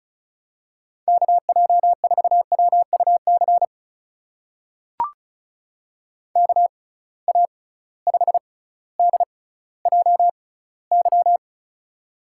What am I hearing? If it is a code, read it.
Morse code, heard as a keyed tone, about 36 wpm, spaced as for 18 wpm, 700 Hz. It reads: KJ4WUC KA5DJY
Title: USA Call Signs - 6 Characters - 6x Character Spacing - 35wpm